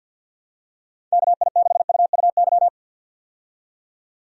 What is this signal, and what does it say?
Morse code, heard as a keyed tone, about 51 wpm, 700 Hz. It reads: KI6UUY
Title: USA Call Signs - 6 Characters - 50wpm